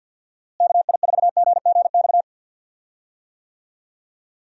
Morse code, heard as a keyed tone, about 50 wpm, 700 Hz. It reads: KI4GGX
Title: USA Call Signs - 6 Characters - Repeated 3x - 50wpm